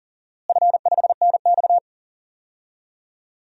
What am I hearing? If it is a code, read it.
Morse code, heard as a keyed tone, about 40 wpm, 700 Hz. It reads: F5NX
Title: DXCC-100 Call Signs - 4 Characters - Repeated 2x - 40wpm